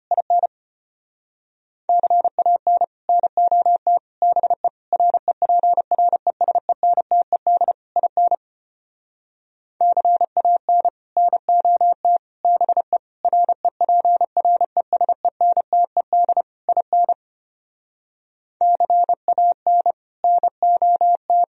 Morse code, heard as a keyed tone, about 31 wpm, 700 Hz. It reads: IN CAN NOT BE REPRESENTED IN CAN NOT BE REPRESENTED IN CAN NOT
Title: Speed-Racing - Sets of 5 Words 25wpm